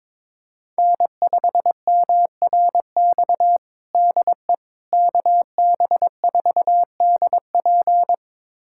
Morse code, heard as a keyed tone, about 22 wpm, 700 Hz. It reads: N5MRX DE KB4DP
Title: USA Callsign DE Callsign - 5 Characters - 22wpm